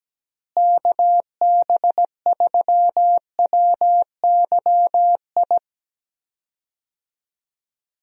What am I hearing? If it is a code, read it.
Morse code, heard as a keyed tone, about 17 wpm, 700 Hz. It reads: KB3WYI
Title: USA Call Signs - 6 Characters - Repeated 3x - 17wpm